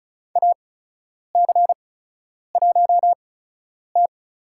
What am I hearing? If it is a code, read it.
Morse code, heard as a keyed tone, about 36 wpm, spaced as for 10 wpm, 700 Hz. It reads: AC1T
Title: USA Call Signs - 8x Character Spacing - 35wpm